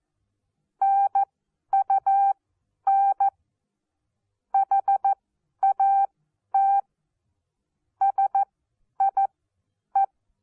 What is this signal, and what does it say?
Morse code, heard as a keyed tone, about 14 wpm, spaced as for 10 wpm, 790 Hz. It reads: NUN HAT SIE
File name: Onlinelehrgang / morsen / DJ4UF Morsen14